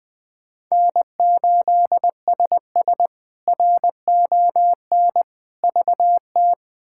Morse code, heard as a keyed tone, about 21 wpm, 700 Hz. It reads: N8SS RON VT
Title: K1USN SST QSO Practice - 20wpm